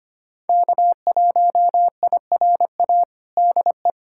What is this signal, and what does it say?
Morse code, heard as a keyed tone, about 25 wpm, 700 Hz. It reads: K1IRA DE
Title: USA Callsign DE Callsign - 5 Characters - 25wpm